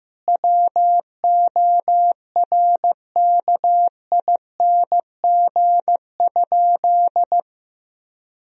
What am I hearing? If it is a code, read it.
Morse code, heard as a keyed tone, about 15 wpm, 700 Hz. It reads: WORKING?
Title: Sentences from Top 1000 Words - Review Part 4 - 15wpm